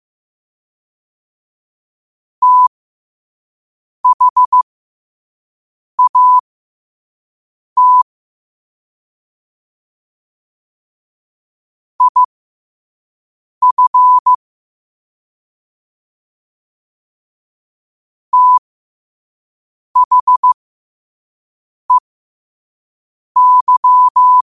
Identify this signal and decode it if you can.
Morse code, heard as a keyed tone, about 15 wpm, spaced as for 5 wpm, 1000 Hz. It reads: THAT IF THEY
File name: qotdfiles / 5wpm